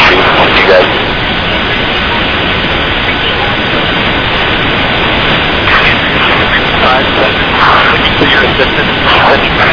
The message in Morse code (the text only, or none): none